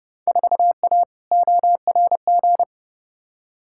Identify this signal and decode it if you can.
Morse code, heard as a keyed tone, about 31 wpm, 700 Hz. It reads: 4A ORG
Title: Field Day Exchanges - 30wpm